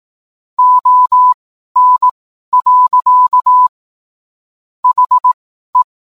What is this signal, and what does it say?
Morse code, heard as a keyed tone, about 18 wpm, spaced as for 12 wpm, 1000 Hz. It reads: ON. HE